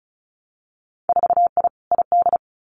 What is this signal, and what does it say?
Morse code, heard as a keyed tone, about 36 wpm, 700 Hz. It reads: 4I ID
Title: Winter Field Day Exchanges 35wpm